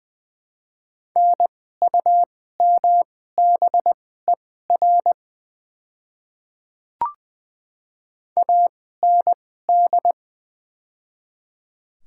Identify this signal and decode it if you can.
Morse code, heard as a keyed tone, about 20 wpm, spaced as for 14 wpm, 690 Hz. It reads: NUMBER AND
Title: 2x Character Spacing - Top 100 Words - 20wpm